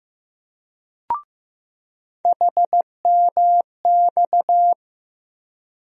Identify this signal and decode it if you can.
Morse code, heard as a keyed tone, about 15 wpm, 700 Hz. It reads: HMX
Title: USA Call Sign Postfixes - 3 Letters 15wpm